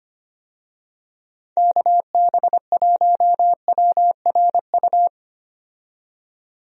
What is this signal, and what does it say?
Morse code, heard as a keyed tone, about 25 wpm, 700 Hz. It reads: KB1WRU
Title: USA Call Signs - 6 Characters - 25wpm